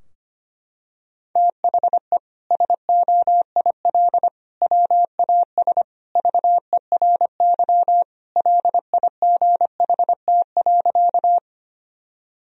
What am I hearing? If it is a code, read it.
Morse code, heard as a keyed tone, about 25 wpm, 700 Hz. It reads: THE SOIL WAS VERY LIGHT.
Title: Sentences from Top 1000 Words - Review Part 4 - 25wpm